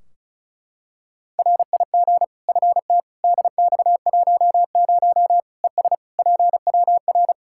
Morse code, heard as a keyed tone, about 36 wpm, 700 Hz. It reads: RIG FT DX10 ES PWR
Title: Practice QSO - Exchange 2 - Info - 35wpm